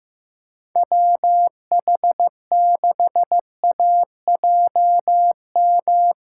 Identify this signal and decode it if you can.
Morse code, heard as a keyed tone, about 15 wpm, 700 Hz. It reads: WH6AJM
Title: USA Callsign DE Callsign 15wpm